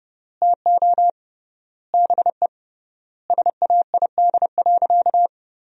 Morse code, heard as a keyed tone, about 32 wpm, 700 Hz. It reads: TO BE SAID.
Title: 3x Word-Spacing - Sentences from Top 500 Words - 30wpm